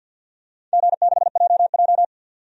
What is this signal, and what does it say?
Morse code, heard as a keyed tone, about 52 wpm, 700 Hz. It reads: G6JJ
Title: DXCC-100 Call Signs - 4 Characters - 50wpm